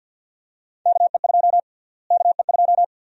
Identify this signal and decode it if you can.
Morse code, heard as a keyed tone, about 53 wpm, 700 Hz. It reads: KE2 KE2
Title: USA Call Sign Prefixes - Repeated 2x - 50wpm